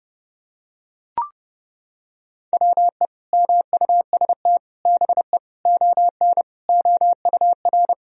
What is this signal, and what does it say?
Morse code, heard as a keyed tone, about 32 wpm, 700 Hz. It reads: WE MUST BE ON OUR
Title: Sentences from Top 1000 Words - Review Part 4 - 30wpm